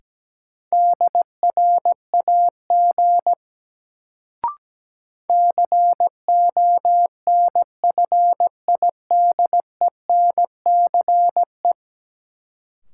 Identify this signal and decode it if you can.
Morse code, heard as a keyed tone, about 17 wpm, 700 Hz. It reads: DRAG CONFIDENCE
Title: Sentences from Top 1500 Words - New Word Drill Alt - 17wpm